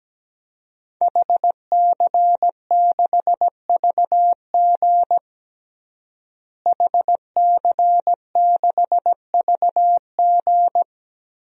Morse code, heard as a keyed tone, about 17 wpm, 700 Hz. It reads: HC6VG HC6VG
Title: DXCC-100 Call Signs - 5 Characters - Repeated 3x - 17wpm